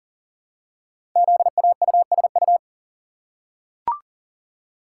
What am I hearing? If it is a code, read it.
Morse code, heard as a keyed tone, about 40 wpm, 700 Hz. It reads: ZAUSU
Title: Code Groups - 5 Letters - 40wpm